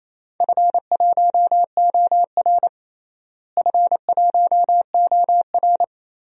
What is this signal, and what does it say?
Morse code, heard as a keyed tone, about 28 wpm, 700 Hz. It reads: F1OR F1OR
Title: DXCC-100 Call Signs - 4 Characters - Repeated 2x - 28wpm